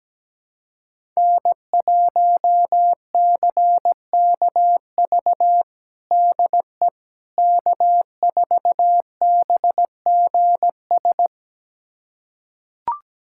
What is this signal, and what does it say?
Morse code, heard as a keyed tone, about 17 wpm, 700 Hz. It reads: N1CKV DE K4BGS
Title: USA Callsign DE Callsign - 5 Characters - 17wpm